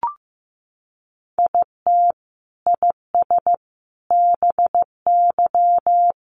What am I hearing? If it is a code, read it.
Morse code, heard as a keyed tone, about 15 wpm, 700 Hz. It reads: IT IS BY
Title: Sentences from Top 1400 Words 15wpm